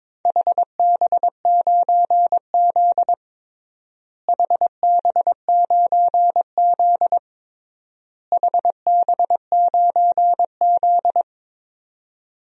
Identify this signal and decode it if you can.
Morse code, heard as a keyed tone, about 22 wpm, 700 Hz. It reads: HB9Z HB9Z HB9Z
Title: DXCC-100 Call Signs - 4 Characters - Repeated 3x - 22wpm